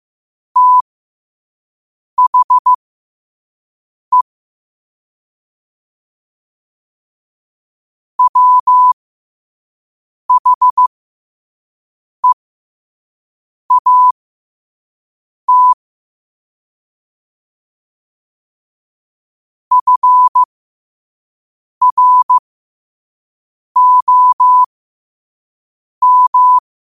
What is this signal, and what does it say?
Morse code, heard as a keyed tone, about 15 wpm, spaced as for 5 wpm, 1000 Hz. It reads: THE WHEAT FROM